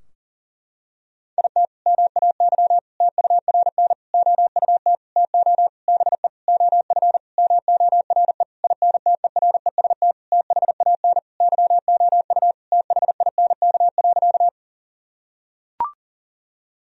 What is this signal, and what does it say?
Morse code, heard as a keyed tone, about 40 wpm, 700 Hz. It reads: IT MAY TURN OUT TO BE OF MORE INTEREST THAN YOU THINK.
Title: Sentences from Top 1000 Words - Review Part 2 - 40wpm